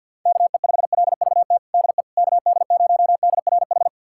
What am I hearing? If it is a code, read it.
Morse code, heard as a keyed tone, about 51 wpm, 700 Hz. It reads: KE5LVT DE KD0DRH
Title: USA Callsign DE Callsign - 6 Characters - 50wpm